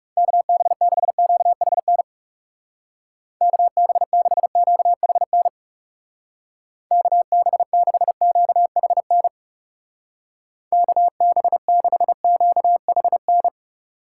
Morse code, heard as a keyed tone, about 39 wpm, 700 Hz. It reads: KB6QHN KB6QHN KB6QHN KB6QHN